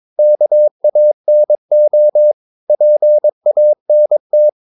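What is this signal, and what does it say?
Morse code, heard as a keyed tone, about 22 wpm, 600 Hz. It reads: KANO PANT
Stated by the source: Morse practice of short Danish words at 22 wpm, from CW med Gnister.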